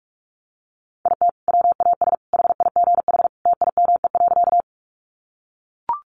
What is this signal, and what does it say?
Morse code, heard as a keyed tone, about 44 wpm, 700 Hz. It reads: IT WAS HIGH TIME.
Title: Sentences from Top 200 Words 45wpm